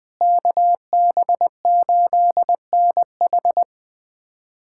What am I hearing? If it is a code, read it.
Morse code, heard as a keyed tone, about 21 wpm, 700 Hz. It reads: KB8NH
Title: USA Callsign DE Callsign - 5 Characters - 20wpm